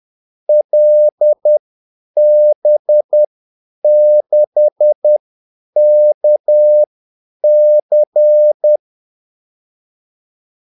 Morse code, heard as a keyed tone, about 10 wpm, spaced as for 8 wpm, 600 Hz. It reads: LB6KC
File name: Callsigns-10-8wpm